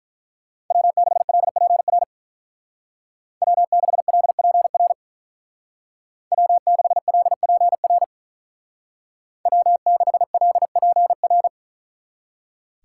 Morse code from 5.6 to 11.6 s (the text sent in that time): W6LPR W6LPR